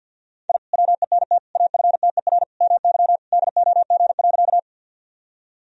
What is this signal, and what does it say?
Morse code, heard as a keyed tone, about 51 wpm, 700 Hz. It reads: I WENT AFTER MY DOG.
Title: Sentences from Top 1000 Words - Review Part 2 - 50wpm